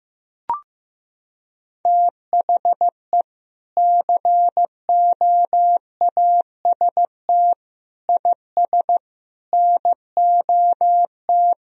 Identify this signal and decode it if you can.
Morse code, heard as a keyed tone, about 15 wpm, 700 Hz. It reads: THE COAST IS NOT
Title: Sentences from Top 1000 Words - Review Part 4 - 15wpm